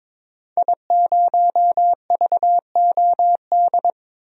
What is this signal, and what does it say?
Morse code, heard as a keyed tone, about 22 wpm, 700 Hz. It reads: I0VOD
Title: DXCC-100 Call Signs - 5 Characters - 22wpm